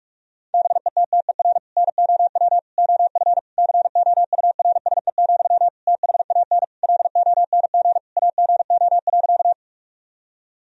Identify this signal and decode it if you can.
Morse code, heard as a keyed tone, about 46 wpm, 700 Hz. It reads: BETTER NOW OF COURSE, THAN LONG AGO.
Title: Sentences from Top 1000 Words - Review Part 3 - 45wpm